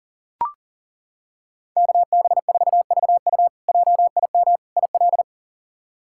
Morse code, heard as a keyed tone, about 37 wpm, 700 Hz. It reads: KB4VU JIM IL
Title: K1USN SST QSO Practice - 40wpm